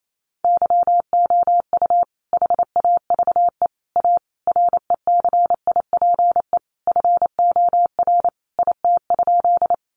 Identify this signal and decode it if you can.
Morse code, heard as a keyed tone, about 28 wpm, 700 Hz. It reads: YOU HAVE A RECIPE FOR IT?
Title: Sentences from Top 1200 Words 28wpm